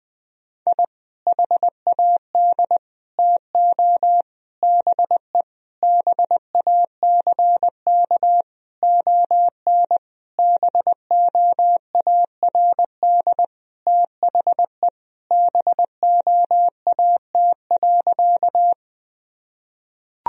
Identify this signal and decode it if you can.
Morse code, heard as a keyed tone, about 20 wpm, 700 Hz. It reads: I HAD TO BE BACK ON BOARD THE BOAT.
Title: Sentences from Top 1000 Words - Review Part 4 - 20wpm